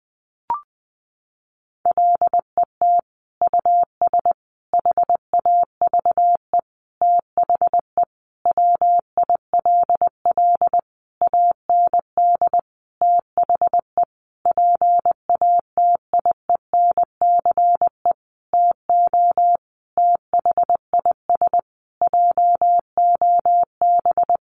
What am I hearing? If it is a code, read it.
Morse code, heard as a keyed tone, about 20 wpm, 700 Hz. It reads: LET US HAVE THE WILL AND THE PATIENCE TO THIS JOB
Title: Sentences from Top 1300 Words 20wpm